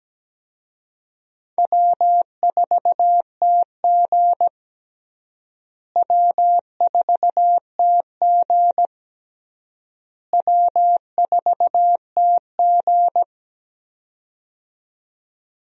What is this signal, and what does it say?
Morse code, heard as a keyed tone, about 17 wpm, 700 Hz. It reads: W4TG W4TG W4TG
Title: USA Call Signs - 4 Characters - Repeated 3x - 17wpm